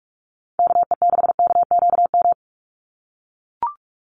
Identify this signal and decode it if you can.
Morse code, heard as a keyed tone, about 44 wpm, 700 Hz. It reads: KE6KQM
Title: USA Call Signs 45wpm